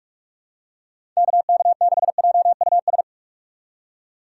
Morse code, heard as a keyed tone, about 46 wpm, 700 Hz. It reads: KK6JUS
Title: USA Call Signs - 6 Characters - 4x Character Spacing - 45wpm